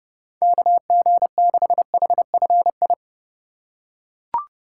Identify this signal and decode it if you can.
Morse code, heard as a keyed tone, about 30 wpm, 700 Hz. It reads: KG6HFI
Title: USA Call Signs - 3x Character Spacing - 30wpm